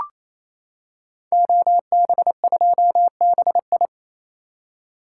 Morse code, heard as a keyed tone, about 29 wpm, 700 Hz. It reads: OB2BI
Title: Code Groups - 5 Letter:Numbers - 28wpm